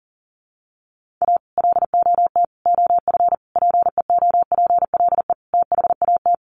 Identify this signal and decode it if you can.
Morse code, heard as a keyed tone, about 40 wpm, 700 Hz. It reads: A LOT OF PEOPLE THAT
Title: Sets of 5 Words Encore2 40wpm